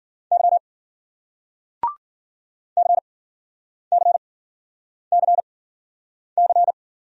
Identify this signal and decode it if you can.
Morse code, heard as a keyed tone, about 53 wpm, spaced as for 9 wpm, 700 Hz. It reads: Y CCCC